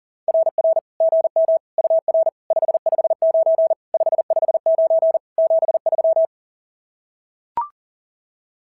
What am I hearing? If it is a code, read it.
Morse code, heard as a keyed tone, about 40 wpm, 650 Hz. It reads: RR GM UR 559 559 73
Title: SOTA QSO Practice 40wpm